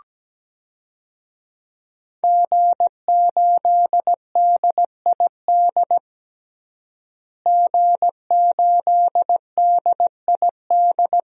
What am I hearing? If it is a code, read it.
Morse code, heard as a keyed tone, about 17 wpm, 700 Hz. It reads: G8DID G8DID
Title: DXCC-100 Call Signs - 5 Characters - Repeated 3x - 17wpm